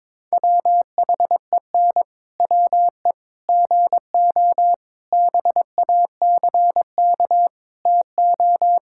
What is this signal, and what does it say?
Morse code, heard as a keyed tone, about 22 wpm, 700 Hz. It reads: WHEN WE GO BACK TO
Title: Pangram Challenge v2 - 22wpm